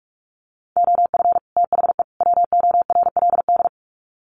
Morse code, heard as a keyed tone, about 44 wpm, 700 Hz. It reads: OF THE WORLD
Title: Sets of 3 Words 45wpm